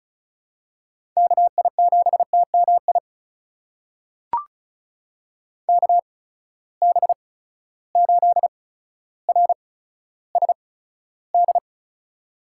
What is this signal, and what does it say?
Morse code, heard as a keyed tone, about 34 wpm, spaced as for 15 wpm, 700 Hz. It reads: KI7TMI KB8RSD
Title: USA Call Signs - 6 Characters - 8x Character Spacing - 35wpm